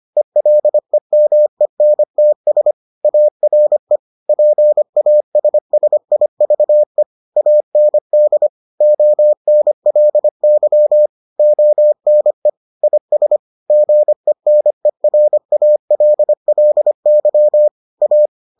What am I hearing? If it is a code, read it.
Morse code, heard as a keyed tone, about 25 wpm, 600 Hz. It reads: ELEMENTS ARE PASSIVE AND ONLY ONE IS GENERALLY A